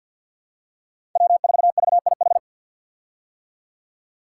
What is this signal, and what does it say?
Morse code, heard as a keyed tone, about 48 wpm, 700 Hz. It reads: W4VIH
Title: Code Groups - 5 Letter:Numbers - 50wpm